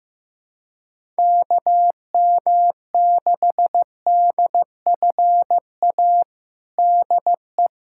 Morse code, heard as a keyed tone, about 16 wpm, 700 Hz. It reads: KM6DFA DE
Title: USA Callsign DE Callsign - 6 Characters - 15wpm